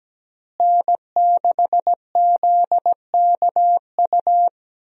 Morse code, heard as a keyed tone, about 17 wpm, 700 Hz. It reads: N6ZKU
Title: USA Call Signs - 5 Characters - Repeated 2x - 17wpm